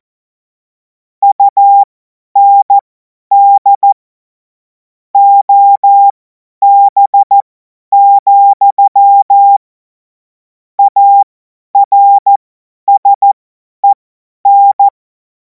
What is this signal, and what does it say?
Morse code, heard as a keyed tone, about 14 wpm, spaced as for 10 wpm, 800 Hz. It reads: UND OB, ARSEN
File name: Morsekurs / Witze / Witz 19. Mai 2026 50 F70